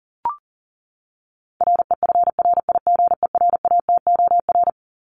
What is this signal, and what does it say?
Morse code, heard as a keyed tone, about 40 wpm, 700 Hz. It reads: REFRIGERATOR
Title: Top 1501-1750 Words 40wpm